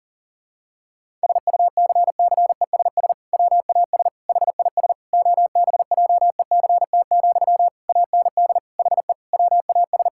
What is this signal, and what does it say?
Morse code, heard as a keyed tone, about 40 wpm, 700 Hz. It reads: SUCCESS WAS HIS OBJECT, AND HE WAS
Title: Sentences from Top 1000 Words - Review Part 6 - 40wpm